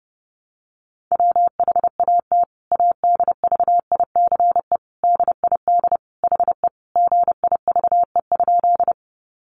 Morse code, heard as a keyed tone, about 30 wpm, 700 Hz. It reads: WHAT ADVICE DID HE GIVE?
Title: Sentences from Top 1100 Words 30wpm